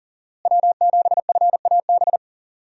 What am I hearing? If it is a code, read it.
Morse code, heard as a keyed tone, about 40 wpm, 700 Hz. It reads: W7FAB
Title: USA Call Signs - 5 Characters - Repeated 3x - 40wpm